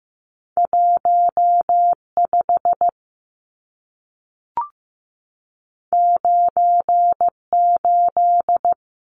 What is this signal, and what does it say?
Morse code, heard as a keyed tone, about 15 wpm, 700 Hz. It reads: 15 98
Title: Numbers - 2-Digit 15wpm